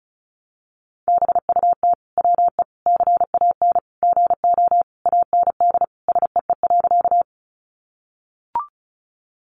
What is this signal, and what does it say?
Morse code, heard as a keyed tone, about 34 wpm, 700 Hz. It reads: BUT WE CAN GO AND SEE.
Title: Sentences from Top 100 Words 35wpm